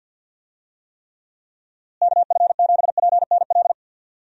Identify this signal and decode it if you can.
Morse code, heard as a keyed tone, about 50 wpm, 700 Hz. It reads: KF7PNL
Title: USA Call Signs - 6 Characters - Repeated 2x - 50wpm